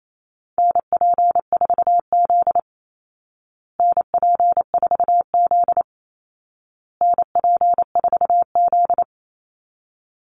Morse code, heard as a keyed tone, about 28 wpm, 700 Hz. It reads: NP4Z NP4Z NP4Z
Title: Call Signs Repeated 2x - 28wpm